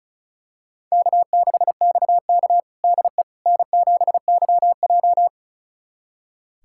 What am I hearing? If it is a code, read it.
Morse code, heard as a keyed tone, about 36 wpm, 700 Hz. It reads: K6XK DE N7YJ
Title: USA Callsign DE Callsign - 4 Characters - 35wpm